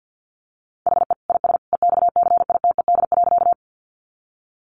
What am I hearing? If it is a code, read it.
Morse code, heard as a keyed tone, about 50 wpm, 700 Hz. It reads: HE IS EXCITED.
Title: Additional Easy Phrases 50wpm